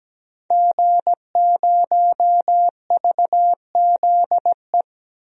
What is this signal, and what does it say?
Morse code, heard as a keyed tone, about 17 wpm, 700 Hz. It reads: G0VZE
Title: DXCC-100 Call Signs - 5 Characters - Repeated 3x - 17wpm